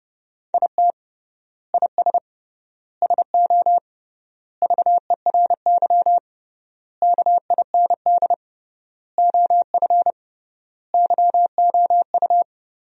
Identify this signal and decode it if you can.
Morse code, heard as a keyed tone, about 30 wpm, 700 Hz. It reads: IT IS SO VERY KIND OF YOU
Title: 3x Word-Spacing - Sentences from Top 300 Words - 30wpm